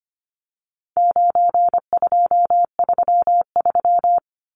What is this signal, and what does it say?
Morse code, heard as a keyed tone, about 25 wpm, 700 Hz. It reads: 9233
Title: Numbers - 4-Digit 25wpm